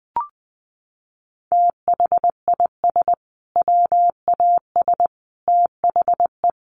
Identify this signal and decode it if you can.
Morse code, heard as a keyed tone, about 20 wpm, 700 Hz. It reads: THIS WAS THE
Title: Sentences from Top 1500 Words 20wpm